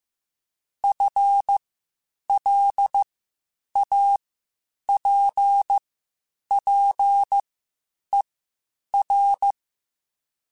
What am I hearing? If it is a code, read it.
Morse code, heard as a keyed tone, about 15 wpm, spaced as for 8 wpm, 800 Hz. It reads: FLAPPER